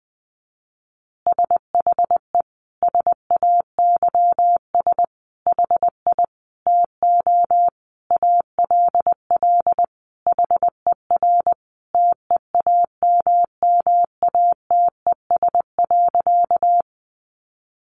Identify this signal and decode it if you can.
Morse code, heard as a keyed tone, about 20 wpm, 700 Hz. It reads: SHE SAYS HI TO ALL HER TEAMMATES.
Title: Easy Phrases 20wpm